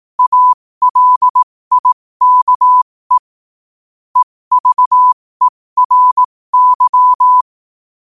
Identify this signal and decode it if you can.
Morse code, heard as a keyed tone, about 18 wpm, spaced as for 15 wpm, 1000 Hz. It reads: ALIKE EVERY
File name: qotdfiles / 15wpm